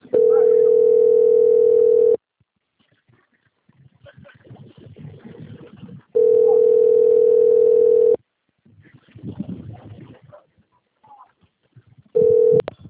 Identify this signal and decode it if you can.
Ringback tone (North America), 440 + 480 Hz.